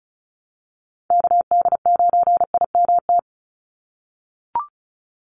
Morse code, heard as a keyed tone, about 34 wpm, 700 Hz. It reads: KD9IMT
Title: USA Call Signs 35wpm